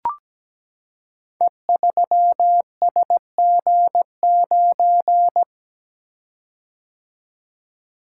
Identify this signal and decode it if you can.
Morse code, heard as a keyed tone, about 17 wpm, 700 Hz. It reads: E3SG9
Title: Code Groups - 5 Letter:Numbers - 17wpm